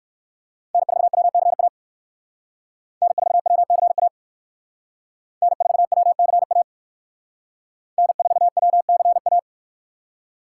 Morse code, heard as a keyed tone, about 57 wpm, 700 Hz. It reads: N4WCA N4WCA N4WCA N4WCA